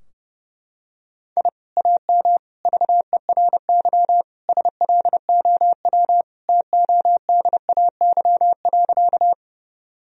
Morse code, heard as a keyed tone, about 30 wpm, 700 Hz. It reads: I AM VERY SLOW TODAY.
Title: Sentences from Top 1000 Words - Review Part 2 - 30wpm